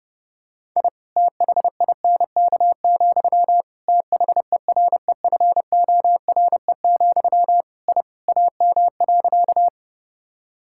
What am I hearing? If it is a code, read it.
Morse code, heard as a keyed tone, about 32 wpm, 700 Hz. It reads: I THINK, THEREFORE, I AM.
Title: Sentences from Top 1000 Words - Review Part 5 - 30wpm